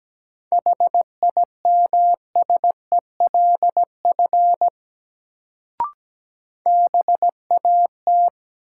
Morse code, heard as a keyed tone, about 17 wpm, 700 Hz. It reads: HIMSELF BAT
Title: Sentences from Top 1500 Words - New Word Drill - 17wpm